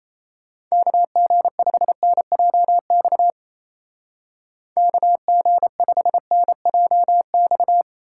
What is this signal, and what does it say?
Morse code, heard as a keyed tone, about 36 wpm, 700 Hz. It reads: KG5NJX KG5NJX